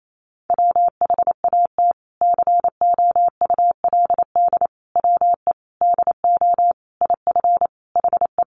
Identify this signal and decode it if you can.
Morse code, heard as a keyed tone, about 28 wpm, 700 Hz. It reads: WHAT COULD WE DO IF HE
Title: Sentences from Top 400 Words 28wpm